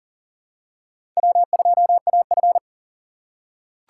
Morse code, heard as a keyed tone, about 40 wpm, 700 Hz. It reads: W2AF
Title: USA Call Signs - 4 Characters - 3x Character Spacing - 40wpm